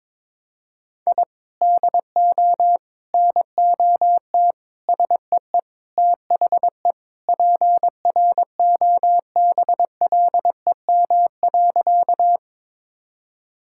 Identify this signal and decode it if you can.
Morse code, heard as a keyed tone, about 22 wpm, 700 Hz. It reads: I DO NOT SEE THE PROBLEM.
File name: practice / 22wpm / Sentences with 4-Word Phrases - 22wpm